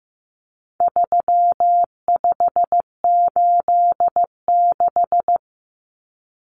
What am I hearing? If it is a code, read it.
Morse code, heard as a keyed tone, about 15 wpm, 700 Hz. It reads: 3586
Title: Numbers - 4-Digit 15wpm